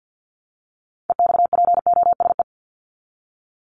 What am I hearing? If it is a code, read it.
Morse code, heard as a keyed tone, about 50 wpm, 700 Hz. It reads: EXPOSE
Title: Top 1501-1750 Words 50wpm